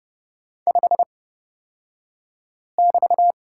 Morse code, heard as a keyed tone, about 30 wpm, spaced as for 5 wpm, 700 Hz. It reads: 5<BT>
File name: Lesson 36d - BT - Two Characters 30wpm